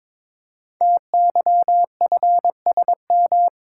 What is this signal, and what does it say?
Morse code, heard as a keyed tone, about 22 wpm, 700 Hz. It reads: TYFSM
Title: Code Groups - 5 Letters - 22wpm